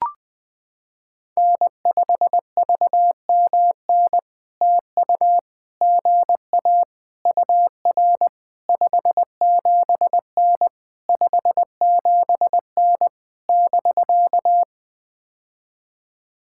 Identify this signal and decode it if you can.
Morse code, heard as a keyed tone, about 20 wpm, 700 Hz. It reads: N5VMN TU GA UR 57N 57N <BK>
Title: POTA QSO Practice - 20wpm